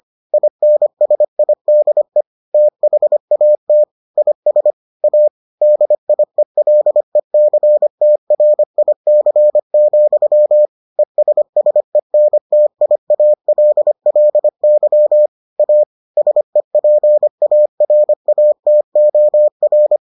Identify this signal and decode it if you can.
Morse code, heard as a keyed tone, about 25 wpm, 600 Hz. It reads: INSIDE THAT IS A DIELECTRIC, ESSENTIALLY A SEPARATOR